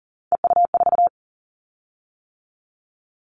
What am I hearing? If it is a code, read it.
Morse code, heard as a keyed tone, about 41 wpm, 700 Hz. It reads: EU4
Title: Callsign Prefixes for Top 100 DXCC - 40wpm